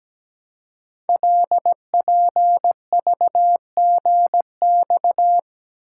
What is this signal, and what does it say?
Morse code, heard as a keyed tone, about 17 wpm, 700 Hz. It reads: LPVGX
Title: Code Groups - 5 Letters - 17wpm